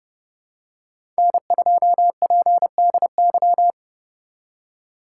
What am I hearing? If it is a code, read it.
Morse code, heard as a keyed tone, about 30 wpm, 700 Hz. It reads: N2PDY
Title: USA Call Signs - 5 Characters - 2x Character Spacing - 30wpm